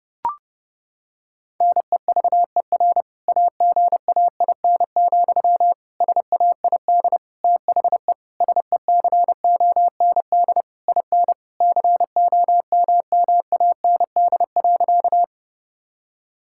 Morse code, heard as a keyed tone, about 32 wpm, 700 Hz. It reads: NEVER AGAIN, SAID THE SECOND IN COMMAND.
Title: Sentences from Top 1600 Words 30wpm